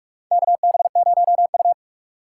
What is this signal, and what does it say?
Morse code, heard as a keyed tone, about 44 wpm, 700 Hz. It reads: KB0U
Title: USA Call Signs - 4 Characters - 3x Character Spacing - 45wpm